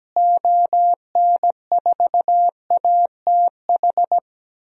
Morse code, heard as a keyed tone, about 17 wpm, 700 Hz. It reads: ON4ATH
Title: DXCC-100 Call Signs - 6 Characters - Repeated 2x - 17wpm